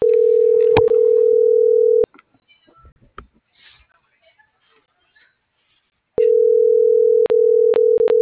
Ringback tone (North America), 440 + 480 Hz.